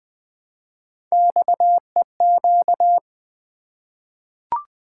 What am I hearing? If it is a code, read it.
Morse code, heard as a keyed tone, about 20 wpm, 700 Hz. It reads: XEQ
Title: Three Letter-Number Combinations - 20wpm